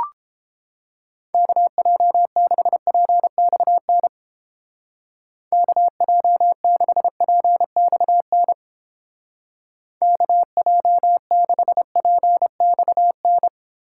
Morse code, heard as a keyed tone, about 29 wpm, 700 Hz. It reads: KJ6PXN KJ6PXN KJ6PXN